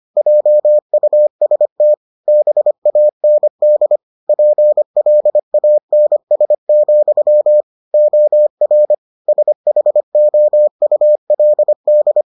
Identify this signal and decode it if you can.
Morse code, heard as a keyed tone, about 25 wpm, 600 Hz. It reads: JUST BAND PLANS, OR SHOULD